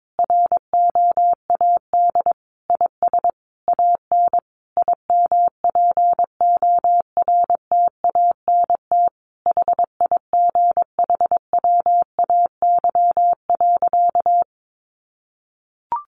Morse code, heard as a keyed tone, about 22 wpm, 700 Hz. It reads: ROAD IS AN IMPORTANT HIGHWAY.
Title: Sentences from Top 1400 Words 22wpm